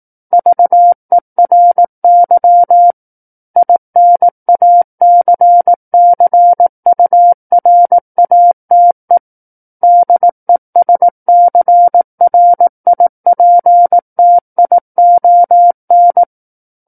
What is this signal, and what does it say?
Morse code, heard as a keyed tone, about 18 wpm, 700 Hz. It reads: VERY INACCURATE DESCRIPTION
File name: 18 WPM